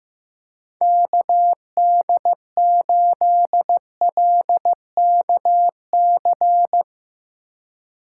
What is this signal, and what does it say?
Morse code, heard as a keyed tone, about 15 wpm, 700 Hz. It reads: KD8LKC
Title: USA Call Signs - 6 Characters - 4x Character Spacing - 15wpm